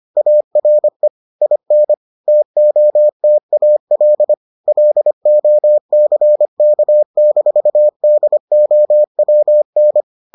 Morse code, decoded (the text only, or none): ARE IN TOTAL LOCK-DOWN